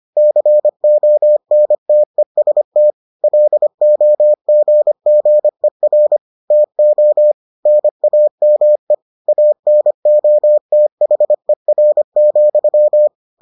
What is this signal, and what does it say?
Morse code, heard as a keyed tone, about 25 wpm, 600 Hz. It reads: CONTEST LOGGER TO NAME ANOTHER,